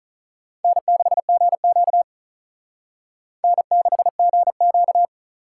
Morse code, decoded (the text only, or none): N6GQ N6GQ